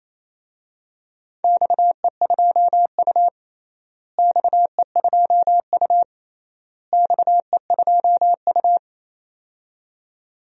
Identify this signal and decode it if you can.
Morse code, heard as a keyed tone, about 28 wpm, 700 Hz. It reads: XE2U XE2U XE2U
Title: DXCC-100 Call Signs - 4 Characters - Repeated 3x - 28wpm